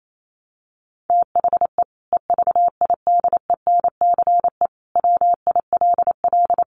Morse code, heard as a keyed tone, about 28 wpm, 700 Hz. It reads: THE EVIDENCE WILL
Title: Sentences from Top 1300 Words 28wpm